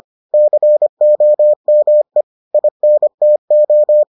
Morse code, heard as a keyed tone, about 25 wpm, 600 Hz. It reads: COME INTO